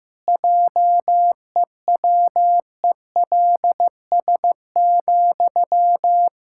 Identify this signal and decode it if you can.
Morse code, heard as a keyed tone, about 15 wpm, 700 Hz. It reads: JEWELS,
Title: Pangram Challenge v2 - 15wpm